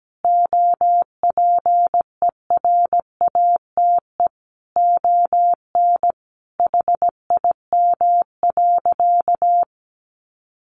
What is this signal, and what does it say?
Morse code, heard as a keyed tone, about 17 wpm, 700 Hz. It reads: OPERATE ON HIM.
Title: Sentences from Top 1500 Words 17wpm